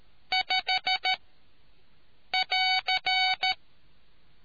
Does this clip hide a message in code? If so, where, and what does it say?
Yes: Morse code from 0.3 to 3.5 s, heard as a keyed tone, about 13 wpm, 750 Hz. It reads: 5 <AR>